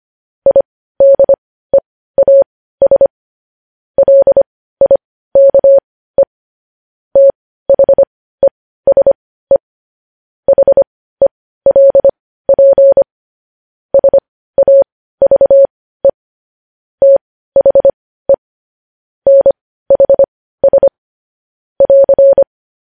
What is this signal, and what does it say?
Morse code, heard as a keyed tone, about 25 wpm, spaced as for 15 wpm, 550 Hz. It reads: IDEAS LIKE THESE HELP SAVE THE NHS <AR>
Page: news-headlines-in-morse-code-at-15-wpm